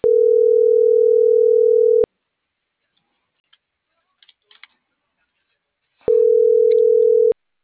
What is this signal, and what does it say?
Ringback tone (North America), 440 + 480 Hz.